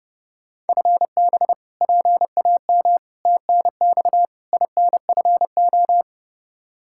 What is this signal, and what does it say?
Morse code, heard as a keyed tone, about 30 wpm, 700 Hz. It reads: FB PAM TNX INFO
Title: Practice QSO - Exchange 3 - More Info - 30wpm